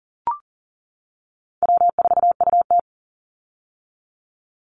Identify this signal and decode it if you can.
Morse code, heard as a keyed tone, about 40 wpm, 700 Hz. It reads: W4UT
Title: Call Signs - 40wpm